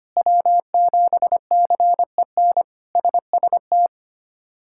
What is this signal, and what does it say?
Morse code, heard as a keyed tone, about 25 wpm, 700 Hz. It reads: W7CEN SST
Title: K1USN SST QSO Practice - 25wpm